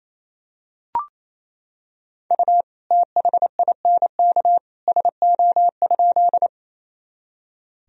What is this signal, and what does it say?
Morse code, heard as a keyed tone, about 27 wpm, 700 Hz. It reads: U THINK SO?
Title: Sentences from Top 200 Words and Abbreviations 28wpm